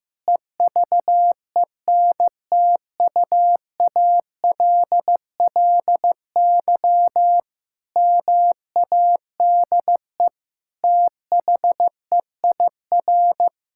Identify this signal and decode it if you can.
Morse code, heard as a keyed tone, about 15 wpm, 700 Hz. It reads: EVENTUALLY MADE THEIR
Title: Sentences from Top 1000 Words - Review Part 6 - 15wpm